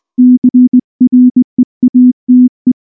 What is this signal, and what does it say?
Morse code, heard as a keyed tone, about 19 wpm, 250 Hz. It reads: CREATE
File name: morse / morse-training-20WPM-1